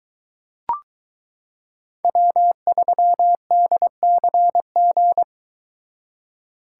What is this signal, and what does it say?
Morse code, heard as a keyed tone, about 22 wpm, 700 Hz. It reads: W3DCG